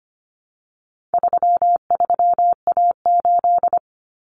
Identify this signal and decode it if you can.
Morse code, heard as a keyed tone, about 25 wpm, 700 Hz. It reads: 33A8
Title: Common Cut Numbers - 4-Digit 25wpm